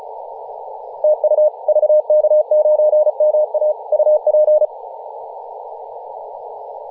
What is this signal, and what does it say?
Morse code, heard as a keyed tone, about 37 wpm, 610 Hz, with radio noise. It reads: TU VK9MA UP
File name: posts / 2017-nov-04-2233z-vk9ma-21-025mhz-cw